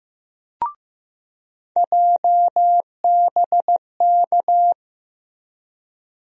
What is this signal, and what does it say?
Morse code, heard as a keyed tone, about 15 wpm, 700 Hz. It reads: JBK
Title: USA Call Sign Postfixes - 3 Letters 15wpm